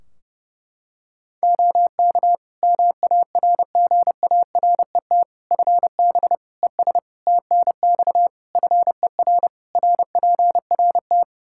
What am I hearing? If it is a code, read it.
Morse code, heard as a keyed tone, about 30 wpm, 700 Hz. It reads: OK MARGARET FB ES TNX FER RPRT
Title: Practice QSO - Exchange 2 - Info - 30wpm